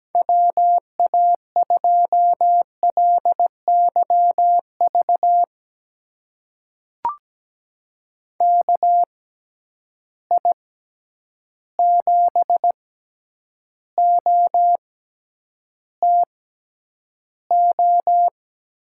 Morse code, heard as a keyed tone, about 17 wpm, spaced as for 9 wpm, 700 Hz. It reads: WA2LYV KI7OTO
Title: USA Call Signs - 6 Characters - 6x Character Spacing - 17wpm